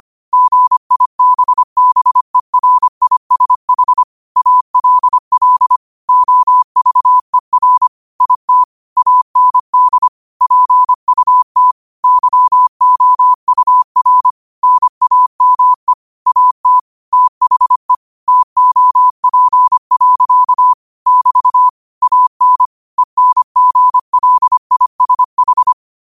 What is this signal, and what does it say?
Morse code, heard as a keyed tone, about 25 wpm, 1000 Hz. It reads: GIBBERISH ALL OVER IT AND PUT YOUR NAME AT THE TOP. <BT> AN ENGLISH